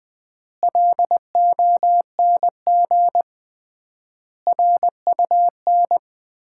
Morse code, heard as a keyed tone, about 20 wpm, 700 Hz. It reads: LONG RUN
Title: 3x Word-Spacing - Sets of 4 Words - 20wpm